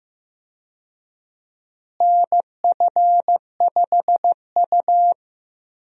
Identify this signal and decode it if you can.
Morse code, heard as a keyed tone, about 15 wpm, 700 Hz. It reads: NF5U
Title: USA Call Signs - 4 Characters - Repeated 2x - 15wpm